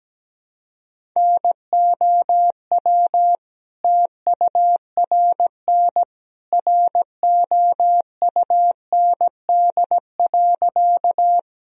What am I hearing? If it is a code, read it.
Morse code, heard as a keyed tone, about 17 wpm, 700 Hz. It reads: NOW TURN ROUND.
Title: Sentences from Top 1000 Words - Review Part 1 - 17wpm